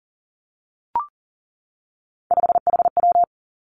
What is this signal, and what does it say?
Morse code, heard as a keyed tone, about 37 wpm, 700 Hz. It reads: 5HW
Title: Three Challenging Letter-Number Combinations 40wpm